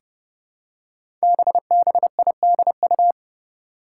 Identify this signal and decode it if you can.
Morse code, heard as a keyed tone, about 30 wpm, 700 Hz. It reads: BBIDU
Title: Five Challenging Letter-Number Combinations 30wpm